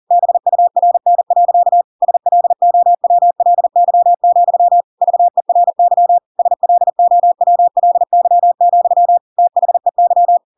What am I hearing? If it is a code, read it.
Morse code, heard as a keyed tone, about 40 wpm, 700 Hz. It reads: BURN. SLOWLY, VERY SLOWLY, THEY
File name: Pinocchio / Pinocchio ch6 40 40 ik1hge